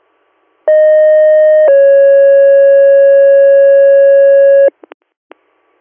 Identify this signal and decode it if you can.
Two-tone page, heard as a one-second tone, then a three-second tone, about 620 then 550 Hz. It is heard from 0.7 to 4.7 s.